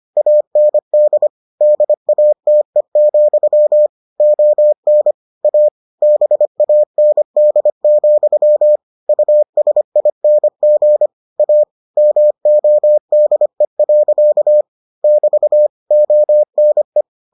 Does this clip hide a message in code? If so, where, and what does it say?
Yes: Morse code from 0.2 to 17.0 s, heard as a keyed tone, about 25 wpm, 600 Hz. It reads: AND DATE, ON A BAND, USING A MODE. <BT> ONE